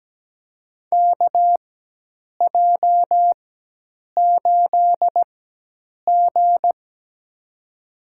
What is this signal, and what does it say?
Morse code, heard as a keyed tone, about 17 wpm, spaced as for 8 wpm, 700 Hz. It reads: KJ8G
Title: USA Call Signs - 4 Characters - 4x Character Spacing - 17wpm